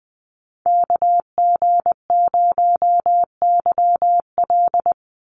Morse code, heard as a keyed tone, about 20 wpm, 700 Hz. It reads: KG0YL
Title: Call Signs - 20wpm